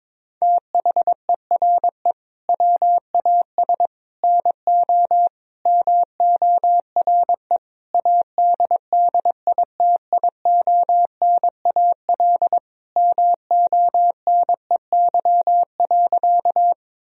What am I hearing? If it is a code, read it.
Morse code, heard as a keyed tone, about 22 wpm, 700 Hz. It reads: THERE WAS NO MORE ADDITIONAL MONEY.
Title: Sentences from Top 1000 Words - Review Part 4 - 22wpm